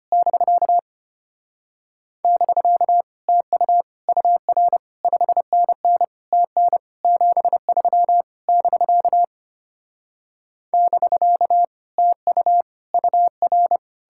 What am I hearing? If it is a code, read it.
Morse code, heard as a keyed tone, about 33 wpm, 700 Hz. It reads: <BK> <BK> TU UR 5NN TN 73 <BK> <BK> TU UR